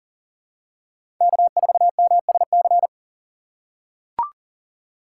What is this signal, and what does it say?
Morse code, heard as a keyed tone, about 40 wpm, 700 Hz. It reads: K4MSC
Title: USA Call Signs - 5 Characters - 3x Character Spacing - 40wpm